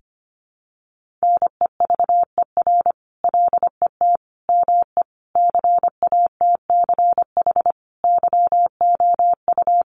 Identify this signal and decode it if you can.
Morse code, heard as a keyed tone, about 25 wpm, 700 Hz. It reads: NEVER LET ME CATCH YOU
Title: Sentences from Top 1500 Words 25wpm